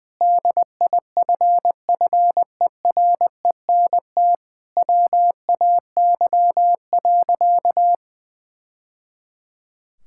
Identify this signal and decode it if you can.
Morse code, heard as a keyed tone, about 21 wpm, 700 Hz. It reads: DIFFERENT WAY.
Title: Sentences from Top 1000 Words - Review Part 6 - 20wpm